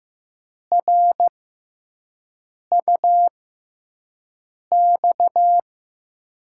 Morse code, heard as a keyed tone, about 15 wpm, spaced as for 5 wpm, 700 Hz. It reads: RUX
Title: USA Call Signs - 6 Characters - 6x Character Spacing - 15wpm